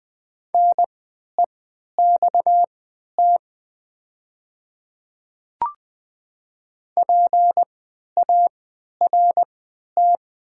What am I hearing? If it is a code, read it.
Morse code, heard as a keyed tone, about 20 wpm, spaced as for 11 wpm, 700 Hz. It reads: NEXT PART
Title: 3x Character Spacing - Top 300 Words - 20wpm